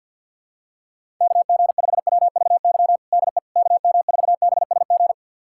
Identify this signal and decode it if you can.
Morse code, heard as a keyed tone, about 51 wpm, 700 Hz. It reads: KG5WVY DE KM4BSG
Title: USA Callsign DE Callsign - 6 Characters - 50wpm